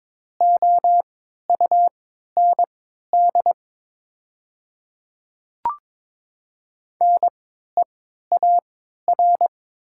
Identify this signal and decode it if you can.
Morse code, heard as a keyed tone, about 22 wpm, spaced as for 12 wpm, 700 Hz. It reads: OUND NEAR
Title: 3x Character Spacing - Top 200 Words - 22wpm